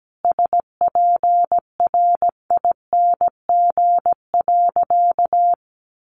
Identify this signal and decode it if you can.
Morse code, heard as a keyed tone, about 17 wpm, 700 Hz. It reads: SPRING.
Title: Sentences from Top 1200 Words 17wpm